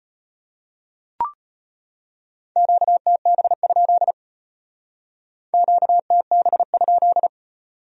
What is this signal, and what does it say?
Morse code, heard as a keyed tone, about 33 wpm, 700 Hz. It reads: QTB? QTB?